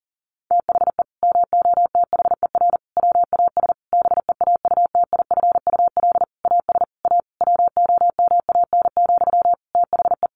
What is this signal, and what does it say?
Morse code, heard as a keyed tone, about 40 wpm, 700 Hz. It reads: THE MOTHER WAS BEAUTIFUL AS A WOMAN, THE
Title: Sentences from Top 1500 Words 40wpm